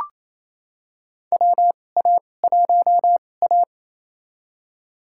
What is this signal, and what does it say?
Morse code, heard as a keyed tone, about 28 wpm, spaced as for 20 wpm, 700 Hz. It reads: WA1A
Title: USA Call Signs - 4 Characters - 2x Character Spacing - 28wpm